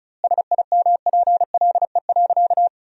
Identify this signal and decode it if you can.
Morse code, heard as a keyed tone, about 34 wpm, 700 Hz. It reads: SIMPLE.
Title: 2x Word-Spacing - Sentences from Top 400 Words - 35wpm